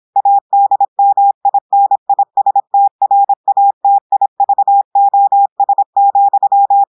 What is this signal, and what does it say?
Morse code, heard as a keyed tone, about 26 wpm, 800 Hz. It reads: ADMINISTRATIVOS,
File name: portugais texte en clair 2 vitesse 26 mots minutes